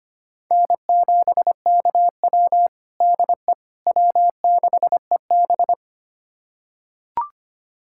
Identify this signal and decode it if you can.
Morse code, heard as a keyed tone, about 25 wpm, 700 Hz. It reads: N7KW DE W6EB
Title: USA Callsign DE Callsign - 4 Characters - 25wpm